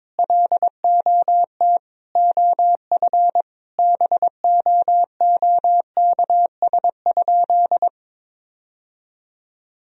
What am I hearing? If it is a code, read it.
Morse code, heard as a keyed tone, about 22 wpm, 700 Hz. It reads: LOT OF BOOKS?
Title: Sentences from Top 1000 Words - Review Part 4 - 22wpm